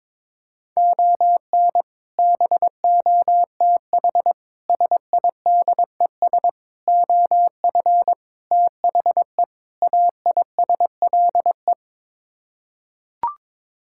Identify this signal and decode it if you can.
Morse code, heard as a keyed tone, about 22 wpm, 700 Hz. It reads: ON BOTH SIDES OF THE AISLE
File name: practice / 22wpm / Sets of 6 Words - 22wpm